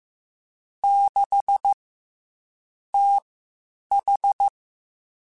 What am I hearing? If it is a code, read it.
Morse code, heard as a keyed tone, about 15 wpm, spaced as for 7 wpm, 800 Hz. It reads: 6TH
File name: CW Comparisons / CW Comparisons-mp3s / CW Comparisons Broken Characters @13